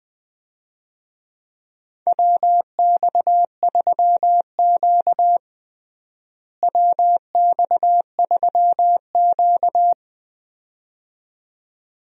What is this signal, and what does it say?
Morse code, heard as a keyed tone, about 20 wpm, 700 Hz. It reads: WX3Q WX3Q
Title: USA Call Signs - 4 Characters - Repeated 2x - 20wpm